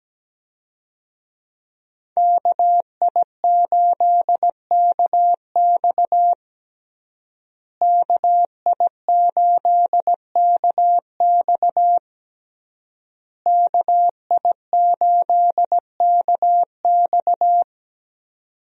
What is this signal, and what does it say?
Morse code, heard as a keyed tone, about 17 wpm, 700 Hz. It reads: KI8KX KI8KX KI8KX
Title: USA Call Signs - 5 Characters - Repeated 3x - 17wpm